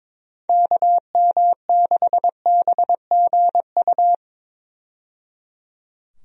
Morse code, heard as a keyed tone, about 22 wpm, 700 Hz. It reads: KM6BGU
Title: USA Call Signs - 6 Characters - 22wpm